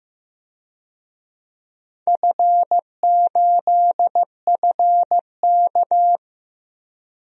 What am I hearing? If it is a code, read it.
Morse code, heard as a keyed tone, about 15 wpm, 700 Hz. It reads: F8FK
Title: DXCC-100 Call Signs - 4 Characters - Repeated 2x - 15wpm